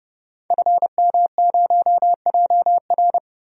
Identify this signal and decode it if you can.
Morse code, heard as a keyed tone, about 32 wpm, 700 Hz. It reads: FM0JR
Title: DXCC-100 Call Signs - 5 Characters - 30wpm